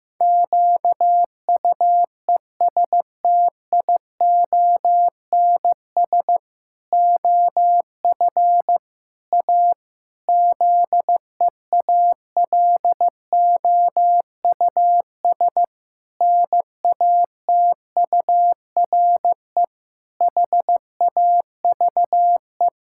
Morse code, heard as a keyed tone, about 15 wpm, 700 Hz. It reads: QUESTIONS OF A ZEALOUS NATURE HAVE
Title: Pangram Challenge v2 - 15wpm